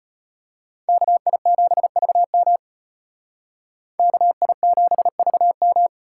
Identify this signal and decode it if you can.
Morse code, heard as a keyed tone, about 33 wpm, 700 Hz. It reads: KI7VM KI7VM